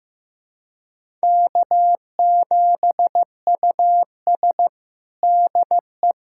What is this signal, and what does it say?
Morse code, heard as a keyed tone, about 15 wpm, 700 Hz. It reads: K7US DE
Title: USA Callsign DE Callsign - 4 Characters - 15wpm